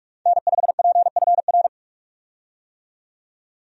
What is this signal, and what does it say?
Morse code, heard as a keyed tone, about 46 wpm, 700 Hz. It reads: N5PFR
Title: USA Call Signs - 5 Characters - 45wpm